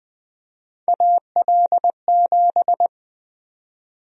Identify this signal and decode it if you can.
Morse code, heard as a keyed tone, about 20 wpm, 700 Hz. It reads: AL7
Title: USA Call Sign Prefixes - 20wpm